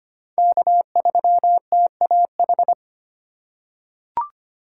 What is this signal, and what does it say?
Morse code, heard as a keyed tone, about 25 wpm, 700 Hz. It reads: K3TAH
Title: USA Call Signs - 5 Characters - 6x Character Spacing - 25wpm